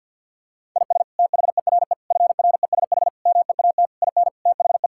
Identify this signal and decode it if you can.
Morse code, heard as a keyed tone, about 50 wpm, 700 Hz. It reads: IS THERE FRESH MEAT IN THE
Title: Sentences from Top 1600 Words 50wpm